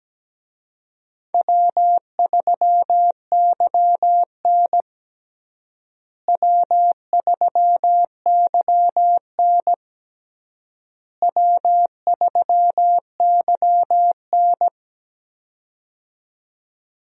Morse code, heard as a keyed tone, about 17 wpm, 700 Hz. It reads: W3YN W3YN W3YN
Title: USA Call Signs - 4 Characters - Repeated 3x - 17wpm